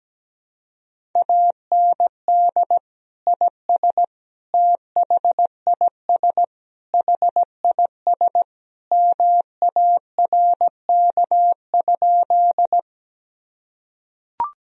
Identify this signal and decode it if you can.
Morse code, heard as a keyed tone, about 17 wpm, 700 Hz. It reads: AND IS THIS HIS MARK?
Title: Sentences from Top 1000 Words - Review Part 1 - 17wpm